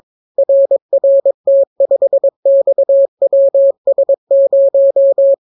Morse code, heard as a keyed tone, about 22 wpm, 550 Hz. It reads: RRT5XWS0